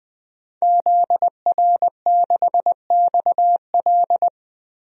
Morse code, heard as a keyed tone, about 20 wpm, 700 Hz. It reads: ZR6XL
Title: DXCC-100 Call Signs - 5 Characters - Repeated 2x - 20wpm